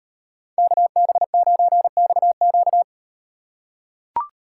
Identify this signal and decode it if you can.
Morse code, heard as a keyed tone, about 38 wpm, 700 Hz. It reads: KB9XQ